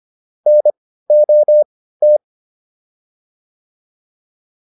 Morse code, heard as a keyed tone, about 25 wpm, spaced as for 15 wpm, 600 Hz. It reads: NOT
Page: intermediate-practice-files